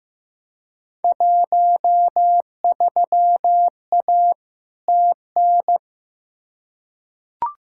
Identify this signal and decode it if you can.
Morse code, heard as a keyed tone, about 15 wpm, 700 Hz. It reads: 13A TN
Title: Field Day Exchanges Encore - 15wpm